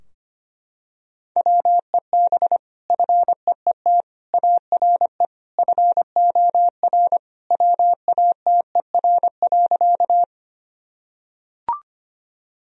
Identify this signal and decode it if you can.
Morse code, heard as a keyed tone, about 25 wpm, 700 Hz. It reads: WEB FEET ARE FOR WATER.
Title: Sentences from Top 1000 Words - Review Part 4 - 25wpm